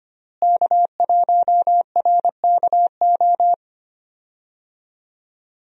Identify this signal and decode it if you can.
Morse code, heard as a keyed tone, about 25 wpm, 700 Hz. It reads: K1RKO
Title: USA Call Signs - 5 Characters - Repeated 2x - 25wpm